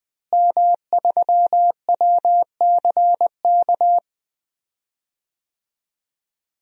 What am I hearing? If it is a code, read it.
Morse code, heard as a keyed tone, about 20 wpm, 700 Hz. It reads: M3WCK
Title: DXCC-100 Call Signs - 5 Characters - Repeated 2x - 20wpm